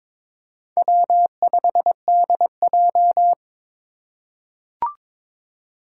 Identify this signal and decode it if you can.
Morse code, heard as a keyed tone, about 22 wpm, 700 Hz. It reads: W5DJ
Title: Four Challenging Letter-Number Combinations 22wpm